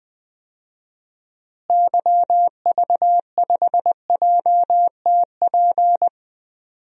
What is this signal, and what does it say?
Morse code, heard as a keyed tone, about 21 wpm, 700 Hz. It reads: YV5JTP
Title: DXCC-100 Call Signs - 6 Characters - Repeated 3x - 20wpm